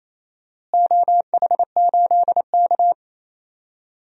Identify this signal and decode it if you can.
Morse code, heard as a keyed tone, about 28 wpm, 700 Hz. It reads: OH8K
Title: DXCC-100 Call Signs - 4 Characters - Repeated 2x - 28wpm